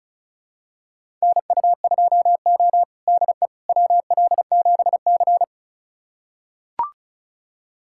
Morse code, heard as a keyed tone, about 36 wpm, 700 Hz. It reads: NU2O DE WL7C
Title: USA Callsign DE Callsign - 4 Characters - 35wpm